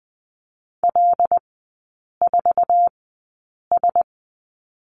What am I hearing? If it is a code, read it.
Morse code, heard as a keyed tone, about 20 wpm, spaced as for 8 wpm, 700 Hz. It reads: L4S
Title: 191 Lesson 31e - Letter v - Three Characters 20wpm